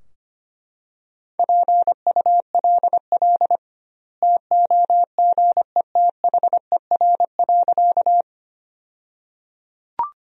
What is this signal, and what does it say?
Morse code, heard as a keyed tone, about 25 wpm, 700 Hz. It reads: PULL TOGETHER.
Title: 2x Word-Spacing - Sentences from Top 400 Words - 25wpm